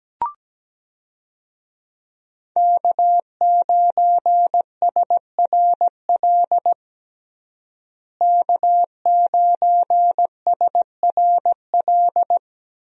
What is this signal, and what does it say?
Morse code, heard as a keyed tone, about 17 wpm, 700 Hz. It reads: K9SRL K9SRL
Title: USA Call Signs - 5 Characters - Repeated 2x - 17wpm